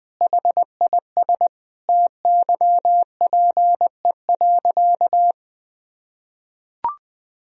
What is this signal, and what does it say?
Morse code, heard as a keyed tone, about 21 wpm, 700 Hz. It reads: HIS TYPE.
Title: Sentences from Top 1000 Words - Review Part 3 - 20wpm